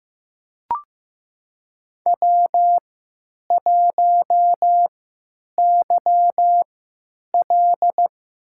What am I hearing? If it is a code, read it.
Morse code, heard as a keyed tone, about 15 wpm, spaced as for 9 wpm, 700 Hz. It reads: W1YL
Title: USA Call Signs - 4 Characters - 3x Character Spacing - 15wpm